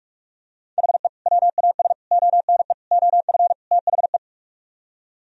Morse code, heard as a keyed tone, about 49 wpm, 700 Hz. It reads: HE WAS ONE OF THE